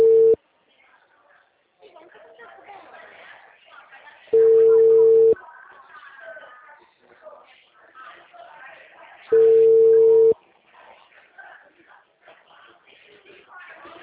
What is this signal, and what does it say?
Ringback tone (China, 450 Hz), with speech.